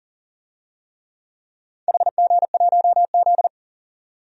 Morse code, heard as a keyed tone, about 38 wpm, 700 Hz. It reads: HG1Z
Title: DXCC-100 Call Signs - 4 Characters - 40wpm